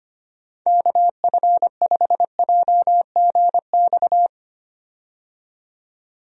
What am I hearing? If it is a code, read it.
Morse code, heard as a keyed tone, about 25 wpm, 700 Hz. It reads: KF5JGX
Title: USA Call Signs - 6 Characters - 25wpm